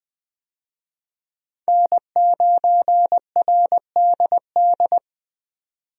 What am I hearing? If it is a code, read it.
Morse code, heard as a keyed tone, about 20 wpm, 700 Hz. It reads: N9RDD
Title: USA Call Signs - 5 Characters - 20wpm